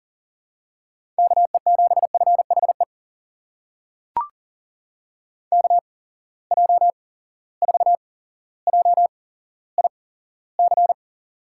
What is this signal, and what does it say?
Morse code, heard as a keyed tone, about 40 wpm, spaced as for 17 wpm, 700 Hz. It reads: KE7FHE KJ4JIC
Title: USA Call Signs - 6 Characters - 8x Character Spacing - 40wpm